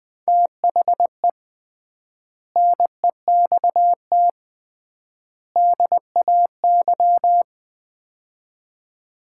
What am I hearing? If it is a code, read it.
Morse code, heard as a keyed tone, about 20 wpm, 700 Hz. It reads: THE NEXT DAY
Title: 3x Word-Spacing - Sets of 3 Words - 20wpm